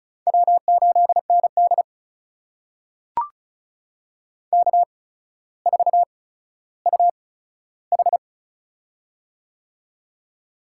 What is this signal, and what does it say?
Morse code, heard as a keyed tone, about 36 wpm, spaced as for 15 wpm, 700 Hz. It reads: W8ND K4UH
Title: USA Call Signs - 4 Characters - 8x Character Spacing - 35wpm